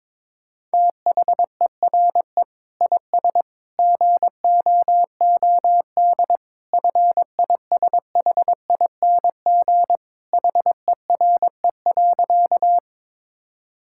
Morse code, heard as a keyed tone, about 22 wpm, 700 Hz. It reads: THERE IS GOOD FISHING HERE.
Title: Sentences from Top 1000 Words - Review Part 5 - 22wpm